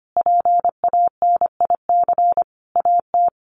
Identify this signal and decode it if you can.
Morse code, heard as a keyed tone, about 25 wpm, 700 Hz. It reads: PANIC AT
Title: A Sound Of Thunder 25wpm